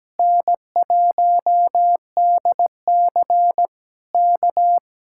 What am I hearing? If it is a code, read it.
Morse code, heard as a keyed tone, about 17 wpm, 700 Hz. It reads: N1DC K
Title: Speed-Racing Common QSO Phrases Encore - 17wpm